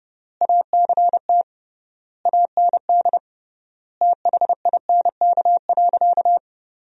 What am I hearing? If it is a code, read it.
Morse code, heard as a keyed tone, about 30 wpm, 700 Hz. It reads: ACT AND THINK.
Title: 3x Word-Spacing - Sentences from Top 500 Words - 30wpm